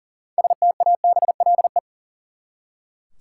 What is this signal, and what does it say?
Morse code, heard as a keyed tone, about 40 wpm, 700 Hz. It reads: STABLE
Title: Six Letter Words - 40wpm